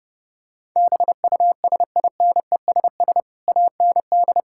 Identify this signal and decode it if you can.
Morse code, heard as a keyed tone, about 30 wpm, 700 Hz. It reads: BUSINESS AND
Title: Sentences from Top 1000 Words - Review Part 5 - 30wpm